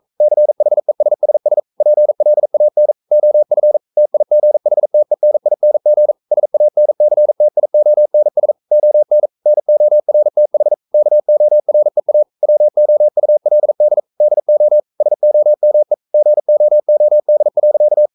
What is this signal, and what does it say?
Morse code, heard as a keyed tone, about 42 wpm, 600 Hz. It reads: CHESSS PLAN OF TIGHTENING SANCTIONS ON NORTH KOREA WOULD DO SOME GOOD.
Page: Romanen Never af Ken Follet på engelsk.